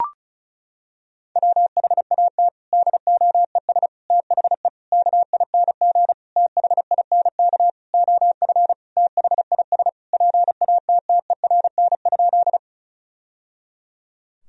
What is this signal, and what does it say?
Morse code, heard as a keyed tone, about 34 wpm, 700 Hz. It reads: WHAT DOES THE KING THINK OF THIS PATTERN?
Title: Sentences from Top 1000 Words - Review Part 2 - 35wpm